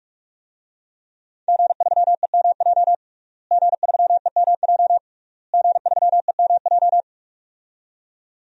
Morse code, heard as a keyed tone, about 41 wpm, 700 Hz. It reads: G3EMJ G3EMJ G3EMJ
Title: DXCC-100 Call Signs - 5 Characters - Repeated 3x - 45wpm